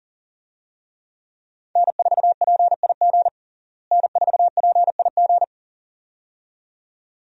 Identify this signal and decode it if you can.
Morse code, heard as a keyed tone, about 40 wpm, 700 Hz. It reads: N4PIG N4PIG
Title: USA Call Signs - 5 Characters - Repeated 2x - 40wpm